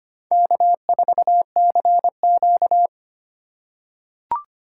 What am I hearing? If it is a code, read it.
Morse code, heard as a keyed tone, about 25 wpm, 700 Hz. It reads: K4CQ
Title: USA Call Signs - 4 Characters - 3x Character Spacing - 25wpm